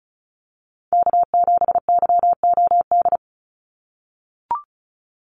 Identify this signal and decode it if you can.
Morse code, heard as a keyed tone, about 34 wpm, 700 Hz. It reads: K7YOD
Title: USA Call Signs 35wpm